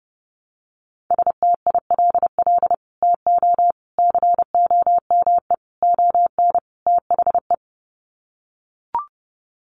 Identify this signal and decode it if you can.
Morse code, heard as a keyed tone, about 30 wpm, 700 Hz. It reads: STILL TO COME ON THE
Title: Sets of 5 Words Encore2 30wpm